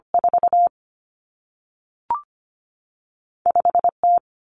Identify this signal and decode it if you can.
Morse code, heard as a keyed tone, about 25 wpm, spaced as for 3 wpm, 700 Hz. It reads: 45 T
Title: Common Cut Numbers - 2-Digit 25wpm